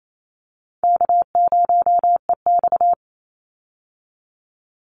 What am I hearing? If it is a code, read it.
Morse code, heard as a keyed tone, about 28 wpm, 700 Hz. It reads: K0EX
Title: Call Signs Repeated 2x - 28wpm